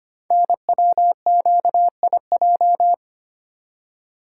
Morse code, heard as a keyed tone, about 25 wpm, 700 Hz. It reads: NWQIJ
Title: Code Groups - 5 Letter:Numbers - 25wpm